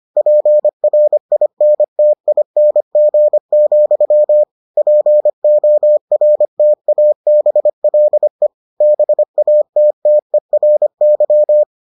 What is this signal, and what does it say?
Morse code, heard as a keyed tone, about 25 wpm, 600 Hz. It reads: PRINTING, PORTABLE BATTERY